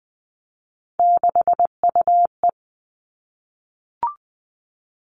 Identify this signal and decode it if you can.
Morse code, heard as a keyed tone, about 20 wpm, 700 Hz. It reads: 6UE
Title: Three Challenging Letter-Number Combinations 20wpm